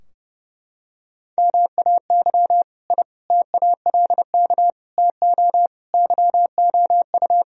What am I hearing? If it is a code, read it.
Morse code, heard as a keyed tone, about 32 wpm, 700 Hz. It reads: MAY I TALK TO YOU
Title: Sentences from Top 1000 Words - Review Part 4 - 30wpm